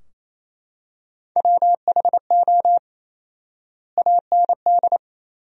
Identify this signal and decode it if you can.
Morse code, heard as a keyed tone, about 28 wpm, 700 Hz. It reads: WHO AND
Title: 4x Word-Spacing - Sentences from Top 100 Words - 28wpm